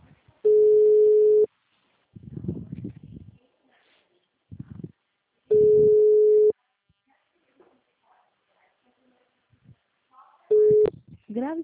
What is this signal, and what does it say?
Ringback tone (Europe, 425 Hz), with speech.
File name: out- -2027- - - .5061